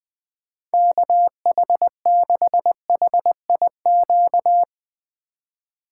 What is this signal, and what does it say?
Morse code, heard as a keyed tone, about 20 wpm, 700 Hz. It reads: KH6HIQ
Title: USA Call Signs - 6 Characters - 2x Character Spacing - 20wpm